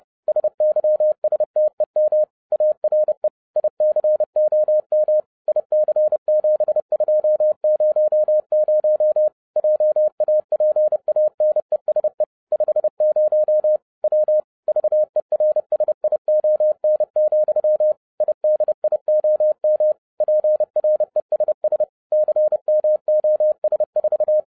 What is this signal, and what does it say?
Morse code, heard as a keyed tone, about 30 wpm, 610 Hz. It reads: SYSTEM ARE ICOM IC7200 JAPANESE 50 W VERSION, IDIOM PRESS CMOS4
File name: 2012-01-mp3 / n11-I DIDNT MISS THE SOLAR CYCLE 24-30wpm